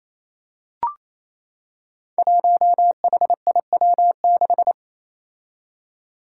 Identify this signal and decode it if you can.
Morse code, heard as a keyed tone, about 28 wpm, 700 Hz. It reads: 1HIW6
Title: Five Challenging Letter-Number Combinations 28wpm